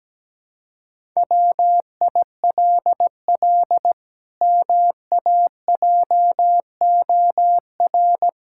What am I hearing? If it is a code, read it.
Morse code, heard as a keyed tone, about 17 wpm, 700 Hz. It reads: WILL MAJOR
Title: Pangram Challenge v2 - 17wpm